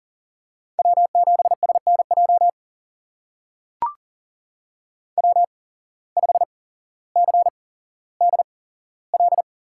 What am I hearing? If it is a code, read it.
Morse code, heard as a keyed tone, about 39 wpm, spaced as for 17 wpm, 700 Hz. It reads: W7SNJ W5CDL
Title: USA Call Signs - 5 Characters - 8x Character Spacing - 40wpm